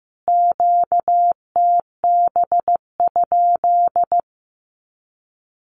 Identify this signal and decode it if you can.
Morse code, heard as a keyed tone, about 15 wpm, 700 Hz. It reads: QTB?
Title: Q-Codes Repeated 3x 15wpm